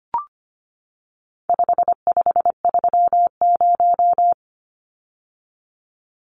Morse code, heard as a keyed tone, about 25 wpm, 700 Hz. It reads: 5530
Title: Numbers - 4-Digit 25wpm